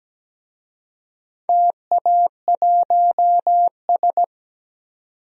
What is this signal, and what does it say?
Morse code, heard as a keyed tone, about 17 wpm, 700 Hz. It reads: TA1S
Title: DXCC-100 Call Signs - 4 Characters - 17wpm